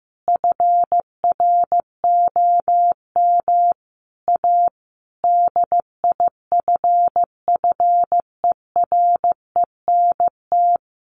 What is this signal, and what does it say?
Morse code, heard as a keyed tone, about 14 wpm, 700 Hz. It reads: FROM A DIFFERENT
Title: Sentences from Top 1500 Words 15wpm